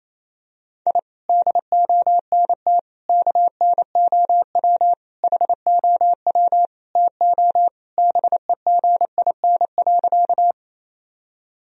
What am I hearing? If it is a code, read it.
Morse code, heard as a keyed tone, about 28 wpm, 700 Hz. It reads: I DONT KNOW HOW TO BEGIN.
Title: Sentences from Top 1000 Words - Review Part 1 - 28wpm